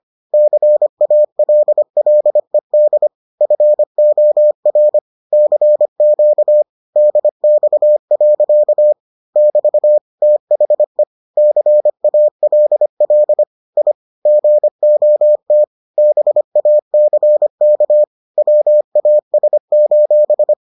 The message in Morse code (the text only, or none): CALLED FOR CQ DX. <BT> THE CALL I GOT BACK WAS: